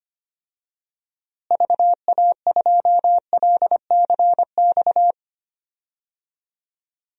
Morse code, heard as a keyed tone, about 25 wpm, 700 Hz. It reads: VA2LCX
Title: DXCC-100 Call Signs - 6 Characters - Repeated 2x - 25wpm